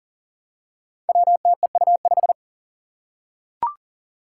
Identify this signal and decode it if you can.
Morse code, heard as a keyed tone, about 39 wpm, 700 Hz. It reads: WTEU5
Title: Five Challenging Letter-Number Combinations 40wpm